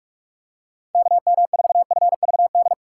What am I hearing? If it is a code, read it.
Morse code, heard as a keyed tone, about 46 wpm, 700 Hz. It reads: KM4FVD
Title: USA Call Signs - 6 Characters - 6x Character Spacing - 45wpm